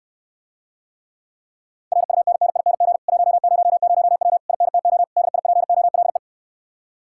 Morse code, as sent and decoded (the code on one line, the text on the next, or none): .-. ..- -. -. .. -. --. / .---- ----- ----- .-- / .. -. - --- / -.. .. .--. --- .-.. .
RUNNING 100W INTO DIPOLE